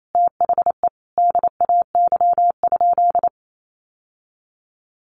Morse code, heard as a keyed tone, about 28 wpm, 700 Hz. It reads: THE DAY?
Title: Sentences from Top 1100 Words 28wpm